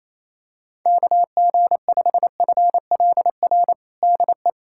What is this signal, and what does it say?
Morse code, heard as a keyed tone, about 28 wpm, 700 Hz. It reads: KG5FLR DE
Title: USA Callsign DE Callsign 28wpm